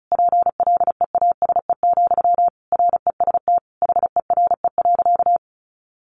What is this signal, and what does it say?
Morse code, heard as a keyed tone, about 34 wpm, 700 Hz. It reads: PLEASE, REST HERE.
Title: Sentences from Top 1500 Words 35wpm